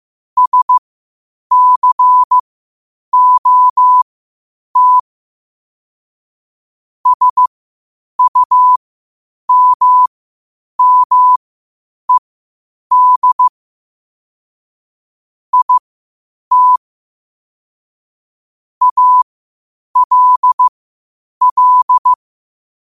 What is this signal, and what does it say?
Morse code, heard as a keyed tone, about 15 wpm, spaced as for 8 wpm, 1000 Hz. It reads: SCOT SUMMED IT ALL
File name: qotdfiles / 8wpm